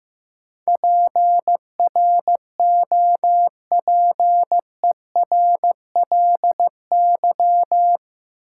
Morse code, heard as a keyed tone, about 15 wpm, 700 Hz. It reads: PROPERLY
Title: Sentences from Top 1100 Words - New Word Drill Alt - 15wpm